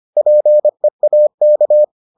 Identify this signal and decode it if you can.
Morse code, heard as a keyed tone, about 25 wpm, 600 Hz. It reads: PEAK